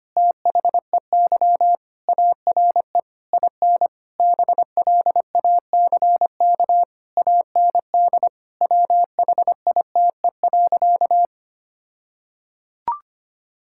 Morse code, heard as a keyed tone, about 25 wpm, 700 Hz. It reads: THEY ARE IN BLACK AND WHITE.
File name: practice / 25wpm / All-Age Appropriate Jokes - 25wpm